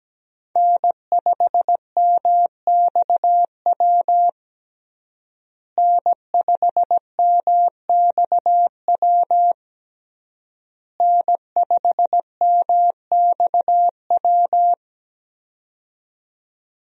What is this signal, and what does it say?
Morse code, heard as a keyed tone, about 17 wpm, 700 Hz. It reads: N5MXW N5MXW N5MXW
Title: USA Call Signs - 5 Characters - Repeated 3x - 17wpm